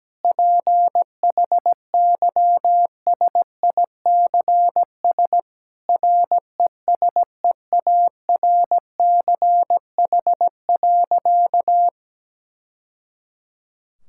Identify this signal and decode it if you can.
Morse code, heard as a keyed tone, about 17 wpm, 700 Hz. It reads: PHYSICS RESEARCH.
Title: Sentences from Top 1000 Words - Review Part 5 - 17wpm